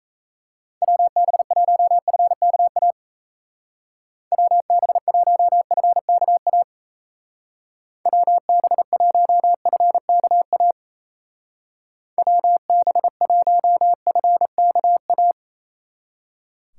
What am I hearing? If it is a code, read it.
Morse code, heard as a keyed tone, about 34 wpm, 700 Hz. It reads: WB1FKA WB1FKA WB1FKA WB1FKA
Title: Speed Racing - USA Call Signs 28wpm